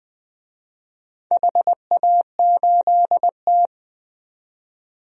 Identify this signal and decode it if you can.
Morse code, heard as a keyed tone, about 20 wpm, 700 Hz. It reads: HA8T
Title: DXCC-100 Call Signs - 4 Characters - 20wpm